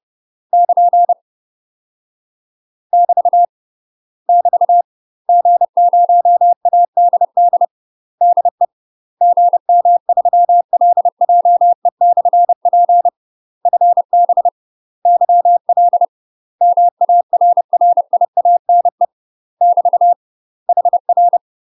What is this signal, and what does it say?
Morse code, heard as a keyed tone, about 30 wpm, 700 Hz. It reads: <KN> <BT> <BT> G0ADD DE GM3LJE/P FB YL MARRIANE <BT> HR
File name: UK MASTER QSOs TRAINING / 30WPM / UK MASTER QSOs 0043